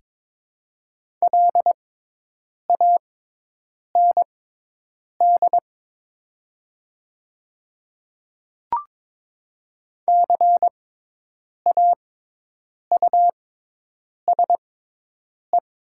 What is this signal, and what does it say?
Morse code, heard as a keyed tone, about 22 wpm, spaced as for 8 wpm, 700 Hz. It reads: LAND CAUSE
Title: 6x Character Spacing - Top 200 Words - 22wpm